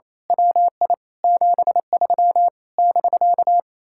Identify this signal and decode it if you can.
Morse code, heard as a keyed tone, about 26 wpm, 700 Hz. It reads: WI 73 <BK>
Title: Speed-Racing Common QSO Phrases - 28wpm